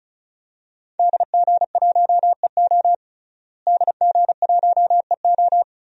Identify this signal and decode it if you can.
Morse code, heard as a keyed tone, about 36 wpm, 700 Hz. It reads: DG1EO DG1EO
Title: DXCC-100 Call Signs - 5 Characters - Repeated 3x - 35wpm